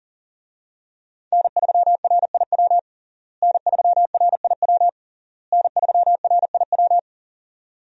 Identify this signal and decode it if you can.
Morse code, heard as a keyed tone, about 40 wpm, 700 Hz. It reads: N3RIW N3RIW N3RIW
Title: USA Call Signs - 5 Characters - Repeated 3x - 40wpm